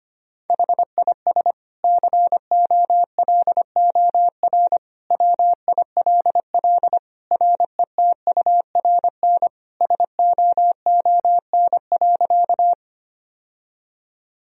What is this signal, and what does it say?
Morse code, heard as a keyed tone, about 25 wpm, 700 Hz. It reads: HIS COLOR WILL RETURN SOON.
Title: Sentences from Top 1000 Words - Review Part 2 - 25wpm